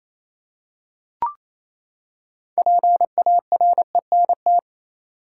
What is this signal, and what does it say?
Morse code, heard as a keyed tone, about 27 wpm, 700 Hz. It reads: PARENT
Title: Six Letter Words - 28wpm